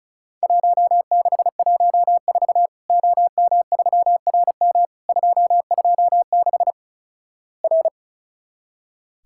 Morse code, heard as a keyed tone, about 36 wpm, 700 Hz. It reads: 1614 OM3RM 226 R
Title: Worked All Europe DX Contest - QTC Exchange - 35wpm